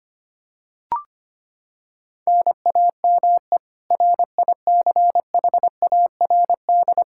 Morse code, heard as a keyed tone, about 25 wpm, 700 Hz. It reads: NAME RICHARD
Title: Practice QSO - Exchange 1 - Beginning - 25wpm